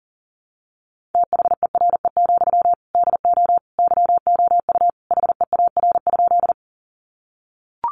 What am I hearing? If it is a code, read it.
Morse code, heard as a keyed tone, about 40 wpm, 700 Hz. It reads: THERE, DO YOU HEAR?
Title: Sentences from Top 300 Words 40wpm